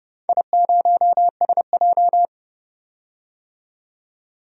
Morse code, heard as a keyed tone, about 30 wpm, 700 Hz. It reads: I0SJ
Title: DXCC-100 Call Signs - 4 Characters - 30wpm